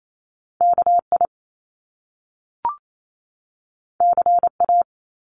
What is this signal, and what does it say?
Morse code, heard as a keyed tone, about 28 wpm, 700 Hz. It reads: KI CA
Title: Top Two Letter Combinations in Words 28wpm